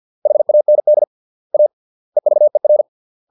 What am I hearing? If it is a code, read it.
Morse code, heard as a keyed tone, about 50 wpm, 600 Hz. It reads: HAND A EVER